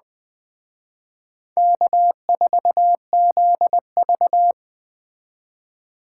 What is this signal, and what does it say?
Morse code, heard as a keyed tone, about 20 wpm, 700 Hz. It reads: K4ZV